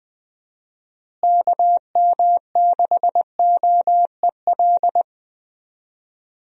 Morse code, heard as a keyed tone, about 20 wpm, 700 Hz. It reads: KM6OEL
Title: USA Call Signs - 6 Characters - Repeated 3x - 20wpm